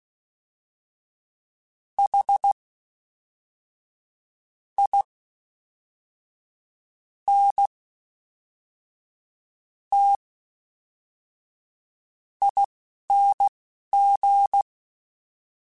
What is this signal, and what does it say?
Morse code, heard as a keyed tone, about 16 wpm, spaced as for 5 wpm, 800 Hz. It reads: HINTING
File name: Special INGs / Special INGs-mp3s / Fast ing @10